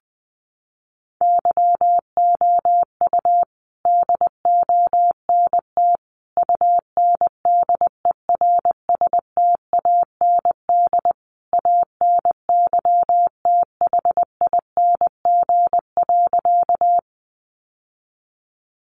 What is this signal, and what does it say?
Morse code, heard as a keyed tone, about 20 wpm, 700 Hz. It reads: YOU DONT UNDERSTAND ANYTHING.
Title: Sentences from Top 1000 Words 20wpm